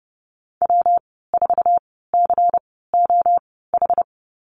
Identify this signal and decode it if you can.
Morse code, heard as a keyed tone, about 30 wpm, spaced as for 17 wpm, 700 Hz. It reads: W4COH
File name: course / 30:10wpm / 116 Lesson 19g - Letter w - Call Signs 30wpm